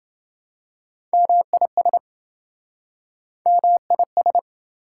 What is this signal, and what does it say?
Morse code, heard as a keyed tone, about 28 wpm, 700 Hz. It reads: MIS MIS